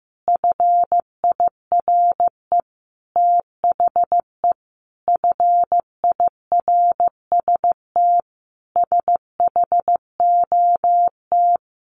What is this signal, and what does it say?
Morse code, heard as a keyed tone, about 15 wpm, 700 Hz. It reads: FIRE THE FIRST SHOT
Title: Sentences from Top 1100 Words 15wpm